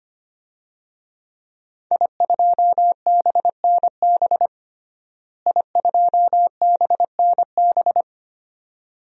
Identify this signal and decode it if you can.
Morse code, heard as a keyed tone, about 25 wpm, 700 Hz. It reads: I2BNB I2BNB
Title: DXCC-100 Call Signs - 5 Characters - Repeated 2x - 25wpm